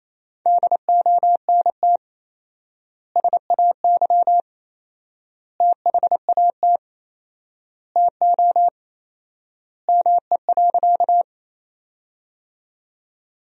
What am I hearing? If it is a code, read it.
Morse code, heard as a keyed tone, about 28 wpm, 700 Hz. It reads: DONT SAY THAT TO ME.
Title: 4x Word-Spacing - Sentences from Top 300 Words - 28wpm